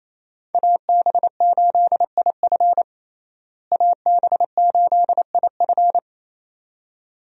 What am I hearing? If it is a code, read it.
Morse code, heard as a keyed tone, about 27 wpm, 700 Hz. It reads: AB8IF AB8IF
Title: USA Call Signs - 5 Characters - Repeated 2x - 28wpm